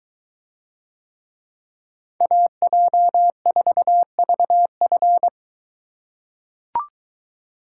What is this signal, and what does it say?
Morse code, heard as a keyed tone, about 22 wpm, 700 Hz. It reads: AJ4VF